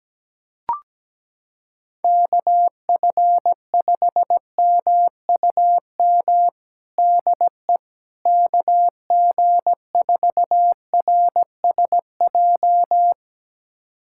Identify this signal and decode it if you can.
Morse code, heard as a keyed tone, about 17 wpm, 700 Hz. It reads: KF5MUM DE KG4RSJ
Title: USA Callsign DE Callsign - 6 Characters - 17wpm